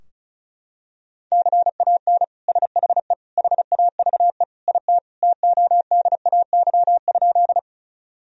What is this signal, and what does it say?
Morse code, heard as a keyed tone, about 36 wpm, 700 Hz. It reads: CAN SHE HAVE IT TODAY?
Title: Sentences from Top 1000 Words - Review Part 4 - 35wpm